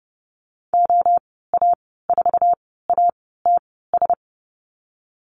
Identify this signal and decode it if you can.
Morse code, heard as a keyed tone, about 30 wpm, spaced as for 17 wpm, 700 Hz. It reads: OA4ATS
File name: course / 30:10wpm / 088 Lesson 15g - Number 5 - Call Signs 30wpm